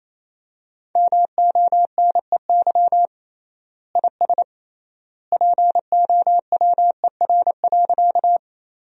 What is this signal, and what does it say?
Morse code, heard as a keyed tone, about 28 wpm, 700 Hz. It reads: MONEY IS POWER.
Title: 3x Word-Spacing - Sentences from Top 500 Words - 28wpm